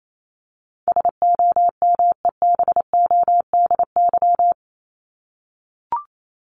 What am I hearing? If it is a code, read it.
Morse code, heard as a keyed tone, about 28 wpm, 700 Hz. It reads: SOMEBODY
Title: Top 1751-2000 Words 28wpm